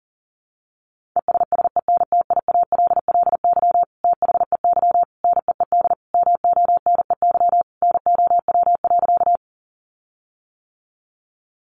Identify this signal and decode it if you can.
Morse code, heard as a keyed tone, about 40 wpm, 700 Hz. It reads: ESSENTIALLY THEY NEED MONEY NOW.
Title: Sentences from Top 1300 Words 40wpm